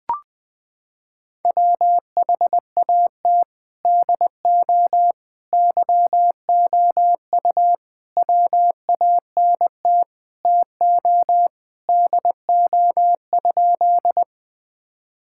Morse code, heard as a keyed tone, about 20 wpm, 700 Hz. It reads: WHAT DO YOU WANT TO DO?
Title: Sentences with 3-Word Phrases - 20wpm